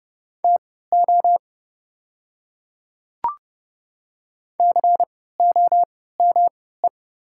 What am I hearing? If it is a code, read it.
Morse code, heard as a keyed tone, about 30 wpm, spaced as for 17 wpm, 700 Hz. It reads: TO COME
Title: 3x Character Spacing - Top 100 Words - 30wpm